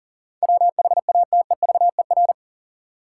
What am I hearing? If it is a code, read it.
Morse code, heard as a keyed tone, about 40 wpm, 700 Hz. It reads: WHATEVER
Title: Sentences from Top 1100 Words - New Word Drill Alt - 40wpm